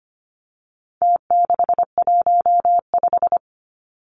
Morse code, heard as a keyed tone, about 25 wpm, 700 Hz. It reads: T615
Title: Numbers - Contest Style - 4-Digit 25wpm